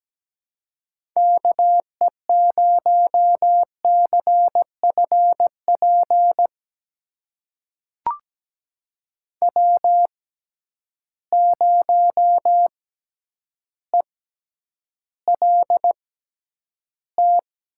Morse code, heard as a keyed tone, about 17 wpm, spaced as for 9 wpm, 700 Hz. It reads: KE0CFP W0ELT
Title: USA Call Signs - 6x Character Spacing - 17wpm